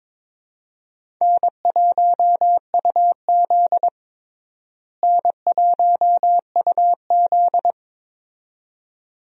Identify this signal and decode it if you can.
Morse code, heard as a keyed tone, about 22 wpm, 700 Hz. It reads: N1UZ N1UZ
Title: USA Call Signs - 4 Characters - Repeated 2x - 22wpm